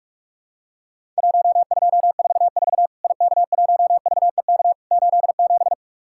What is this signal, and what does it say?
Morse code, heard as a keyed tone, about 44 wpm, 700 Hz. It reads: 1244 IK1VEK 87
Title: Worked All Europe DX Contest - QTC Exchange - 45wpm